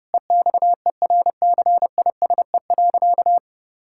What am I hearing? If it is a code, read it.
Morse code, heard as a keyed tone, about 30 wpm, 700 Hz. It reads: EXERCISE.
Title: Sentences from Top 1000 Words - Review Part 6 - 30wpm